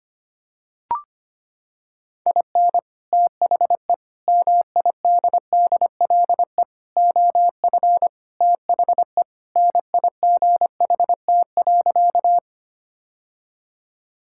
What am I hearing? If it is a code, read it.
Morse code, heard as a keyed tone, about 25 wpm, 700 Hz. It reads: IN THE MIDDLE OF THE NIGHT.
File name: practice / 25wpm / Sentences with 3-Word Phrases - 25wpm